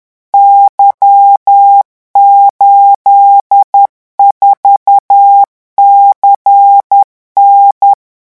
Morse code, heard as a keyed tone, about 10 wpm, 800 Hz. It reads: Y84CN
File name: morse mp3 / revision13-vitesse 10 mots